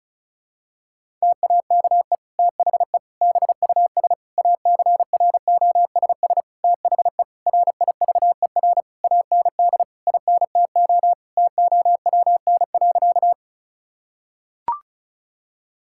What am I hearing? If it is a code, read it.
Morse code, heard as a keyed tone, about 34 wpm, 700 Hz. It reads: TAKE THE BUS ACROSS THE RIVER AND INTO TOWN.
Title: Sentences from Top 1000 Words - Review Part 6 - 35wpm